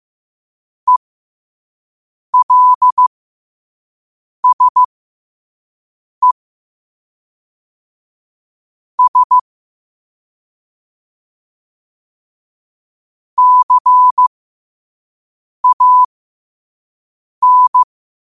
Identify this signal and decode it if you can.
Morse code, heard as a keyed tone, about 15 wpm, spaced as for 5 wpm, 1000 Hz. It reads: ELSE S CAN
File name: qotdfiles / 5wpm